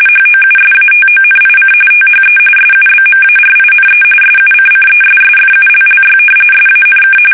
RTTY, about 45.45 baud. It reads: HOUSATONIC THE SUBMARINE SUCCESSFULLY SANK
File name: fsk850 / usnavy3